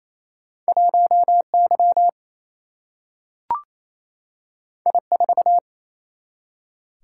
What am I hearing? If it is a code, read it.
Morse code, heard as a keyed tone, about 28 wpm, 700 Hz. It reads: 1 Y I 4
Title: Single-Letter Single-Number 28wpm